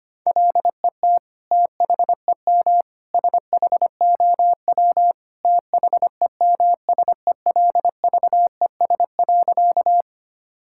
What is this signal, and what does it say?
Morse code, heard as a keyed tone, about 25 wpm, 700 Hz. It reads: LET THEM SHOW THEMSELVES.
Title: Sentences from Top 1000 Words - Review Part 5 - 25wpm